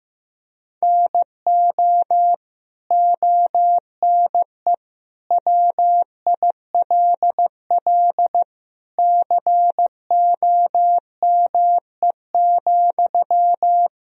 Morse code, heard as a keyed tone, about 15 wpm, 700 Hz. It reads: NO ONE WILL COME,
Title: Sentences from Top 100 Words and Abbreviations 15wpm